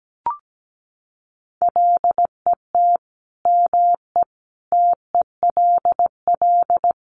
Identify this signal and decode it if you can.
Morse code, heard as a keyed tone, about 17 wpm, 700 Hz. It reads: LET ME TELL
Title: Sets of 4 Words 17wpm